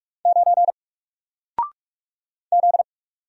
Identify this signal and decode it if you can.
Morse code, heard as a keyed tone, about 44 wpm, 700 Hz. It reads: 9 Z
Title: Single Character (Letter:Number:Punctuation:Prosign) Alt 45wpm